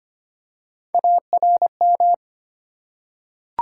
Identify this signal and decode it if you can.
Morse code, heard as a keyed tone, about 25 wpm, 700 Hz. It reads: ARM
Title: Three Letter Words - 25wpm